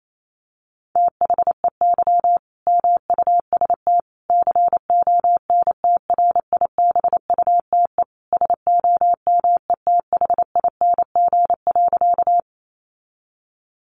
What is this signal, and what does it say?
Morse code, heard as a keyed tone, about 28 wpm, 700 Hz. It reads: THEY MUST CONTRIBUTE SOMETHING.
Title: Sentences from Top 1400 Words 28wpm